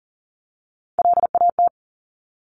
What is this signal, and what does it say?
Morse code, heard as a keyed tone, about 41 wpm, 700 Hz. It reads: LAT
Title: Top Three Letter Combinations in Words 40wpm